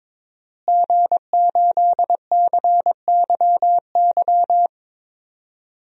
Morse code, heard as a keyed tone, about 22 wpm, 700 Hz. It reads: G8CYY
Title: DXCC-100 Call Signs - 5 Characters - Repeated 3x - 22wpm